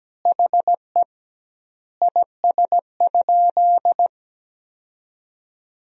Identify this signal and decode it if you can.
Morse code, heard as a keyed tone, about 17 wpm, 700 Hz. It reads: HE IS?
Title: 2x Word-Spacing - Sentences from Top 300 Words - 17wpm